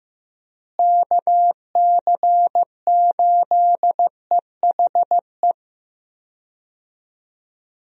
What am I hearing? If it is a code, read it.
Morse code, heard as a keyed tone, about 15 wpm, 700 Hz. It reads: KC8EHE
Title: USA Call Signs - 6 Characters - Repeated 3x - 15wpm